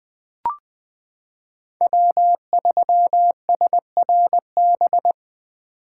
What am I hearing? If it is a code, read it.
Morse code, heard as a keyed tone, about 21 wpm, 700 Hz. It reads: W3SRB
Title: 2023 POTA Activator Call Signs - 20wpm